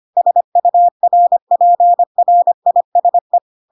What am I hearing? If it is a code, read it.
Morse code, heard as a keyed tone, about 25 wpm, 700 Hz. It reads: SURPRISE
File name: Pinocchio / Pinocchio ch3 25 25 ik1hge